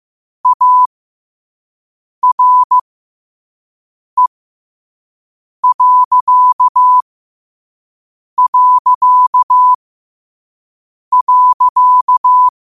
Morse code, heard as a keyed tone, about 15 wpm, spaced as for 5 wpm, 1000 Hz. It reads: ARE...
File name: qotdfiles / 5wpm